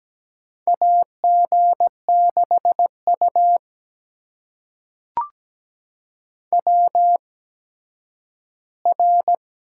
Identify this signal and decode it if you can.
Morse code, heard as a keyed tone, about 17 wpm, spaced as for 10 wpm, 700 Hz. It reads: AG6U WR
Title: USA Call Signs - 4 Characters - 8x Character Spacing - 17wpm